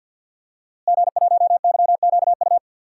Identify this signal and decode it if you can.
Morse code, heard as a keyed tone, about 49 wpm, 700 Hz. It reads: G1YQU
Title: DXCC-100 Call Signs - 5 Characters - Repeated 2x - 50wpm